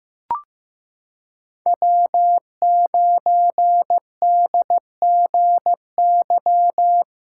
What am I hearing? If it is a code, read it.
Morse code, heard as a keyed tone, about 15 wpm, 700 Hz. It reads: W9DGY
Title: USA Callsign DE Callsign - 5 Characters - 15wpm